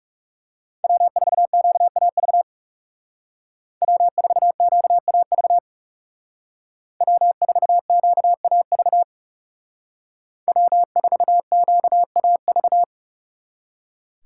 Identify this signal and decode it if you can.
Morse code, heard as a keyed tone, about 39 wpm, 700 Hz. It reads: W4QAV W4QAV W4QAV W4QAV